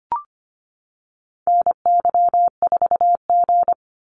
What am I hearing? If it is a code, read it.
Morse code, heard as a keyed tone, about 25 wpm, 700 Hz. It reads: NY4G
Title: NA SOTA Activator Callsigns 25wpm